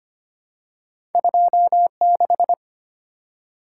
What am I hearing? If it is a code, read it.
Morse code, heard as a keyed tone, about 25 wpm, 700 Hz. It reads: 26
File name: Two Letter-Number Combinations - 25wpm